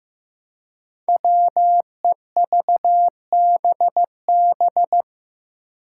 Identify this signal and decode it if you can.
Morse code, heard as a keyed tone, about 15 wpm, 700 Hz. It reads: WEVBB
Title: Five Challenging Letter-Number Combinations 15wpm